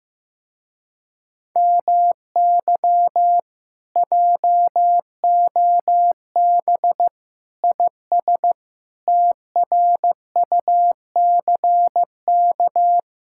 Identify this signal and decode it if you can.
Morse code, heard as a keyed tone, about 15 wpm, 700 Hz. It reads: MY JOB IS TRUCK
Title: Speed-Racing Common QSO Phrases Encore - 15wpm